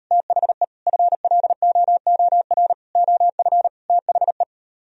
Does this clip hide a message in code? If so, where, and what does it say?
Yes: Morse code from 0.1 to 4.4 s, heard as a keyed tone, about 42 wpm, 700 Hz. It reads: THE FLOOR OF THE